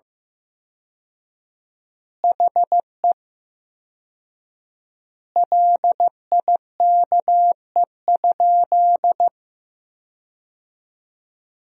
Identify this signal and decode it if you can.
Morse code, heard as a keyed tone, about 16 wpm, 700 Hz. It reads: HE LIKE?
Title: 4x Word-Spacing - Sentences from Top 100 Words - 15wpm